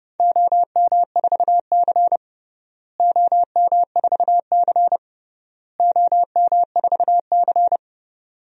Morse code, heard as a keyed tone, about 30 wpm, 700 Hz. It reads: OM4C OM4C OM4C
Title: DXCC-100 Call Signs - 4 Characters - Repeated 3x - 30wpm